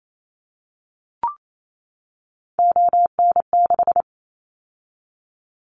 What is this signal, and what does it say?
Morse code, heard as a keyed tone, about 28 wpm, 700 Hz. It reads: ON6
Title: Callsign Prefixes for Top 50 DXCC - 28wpm